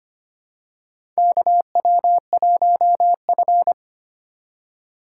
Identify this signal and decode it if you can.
Morse code, heard as a keyed tone, about 25 wpm, 700 Hz. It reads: KW1F
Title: USA Call Signs - 4 Characters - 2x Character Spacing - 25wpm